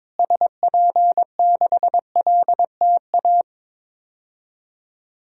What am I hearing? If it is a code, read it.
Morse code, heard as a keyed tone, about 22 wpm, 700 Hz. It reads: SP6LTA
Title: DXCC-100 Call Signs - 6 Characters - Repeated 3x - 22wpm